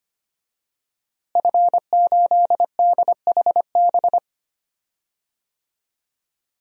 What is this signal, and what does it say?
Morse code, heard as a keyed tone, about 25 wpm, 700 Hz. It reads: F8DHB
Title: DXCC-100 Call Signs - 5 Characters - Repeated 3x - 25wpm